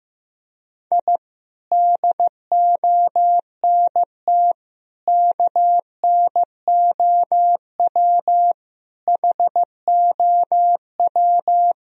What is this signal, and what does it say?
Morse code, heard as a keyed tone, about 15 wpm, 700 Hz. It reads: I DONT KNOW HOW
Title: Sentences from Top 1000 Words - Review Part 1 - 15wpm